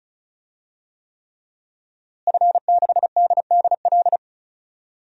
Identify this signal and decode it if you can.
Morse code, heard as a keyed tone, about 34 wpm, 700 Hz. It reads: F6DDL
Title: DXCC-100 Call Signs - 5 Characters - Repeated 3x - 35wpm